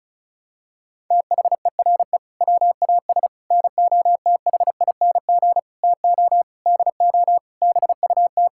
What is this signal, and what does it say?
Morse code, heard as a keyed tone, about 36 wpm, 700 Hz. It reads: THERE WAS NOTHING TO DO BUT
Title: Sentences from Top 1000 Words - Review Part 6 - 35wpm